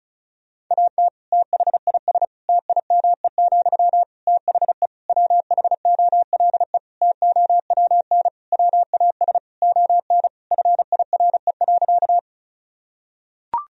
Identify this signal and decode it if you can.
Morse code, heard as a keyed tone, about 34 wpm, 700 Hz. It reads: AT THIS TIME, THE WHOLE TOWN WAS ON FIRE.
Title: Sentences from Top 1000 Words - Review Part 2 - 35wpm